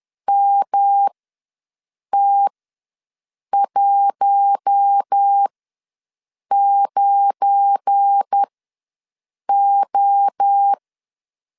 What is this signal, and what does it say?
Morse code, heard as a keyed tone, about 10 wpm, spaced as for 6 wpm, 800 Hz. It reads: MT19O
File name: LEZIONI / 8 YJ45 / 8 Y ws 10wpm 3952